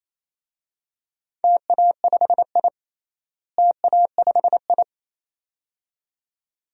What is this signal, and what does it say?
Morse code, heard as a keyed tone, about 28 wpm, 700 Hz. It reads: TA5I TA5I
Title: DXCC-100 Call Signs - 4 Characters - Repeated 2x - 28wpm